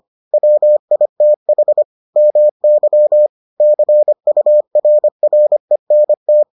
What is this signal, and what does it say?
Morse code, heard as a keyed tone, about 25 wpm, 600 Hz. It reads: WITH MY CURRENT